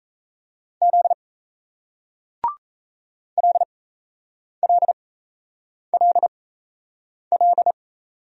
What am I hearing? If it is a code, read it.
Morse code, heard as a keyed tone, about 39 wpm, spaced as for 8 wpm, 700 Hz. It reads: Z LLLL